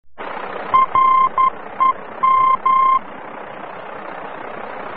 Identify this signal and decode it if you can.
Morse code, heard as a keyed tone, about 11 wpm, 1050 Hz. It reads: REM
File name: france / mx / REM-VOR-F